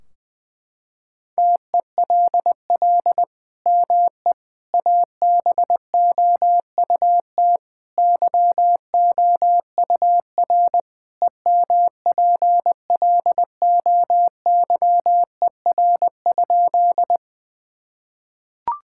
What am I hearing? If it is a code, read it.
Morse code, heard as a keyed tone, about 20 wpm, 700 Hz. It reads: TELL ME ABOUT YOUR EMPLOYER?
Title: Sentences from Top 1600 Words 20wpm